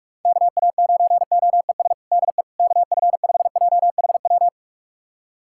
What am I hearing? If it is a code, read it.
Morse code, heard as a keyed tone, about 42 wpm, 700 Hz. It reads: KA9OES DE KF5JHW
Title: USA Callsign DE Callsign 45wpm